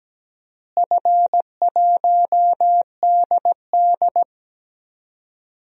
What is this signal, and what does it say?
Morse code, heard as a keyed tone, about 17 wpm, 700 Hz. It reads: F1DD
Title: DXCC-100 Call Signs - 4 Characters - Repeated 2x - 17wpm